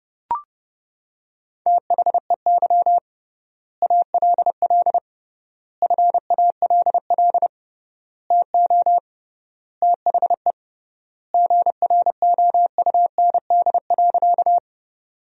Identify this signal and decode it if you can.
Morse code, heard as a keyed tone, about 29 wpm, 700 Hz. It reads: THEY ALL FALL TO THE GROUND.
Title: 3x Word-Spacing - Sentences from Top 500 Words - 30wpm